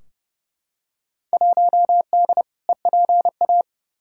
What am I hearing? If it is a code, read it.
Morse code, heard as a keyed tone, about 31 wpm, 700 Hz. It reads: 1D EPA
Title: Field Day Exchanges - 30wpm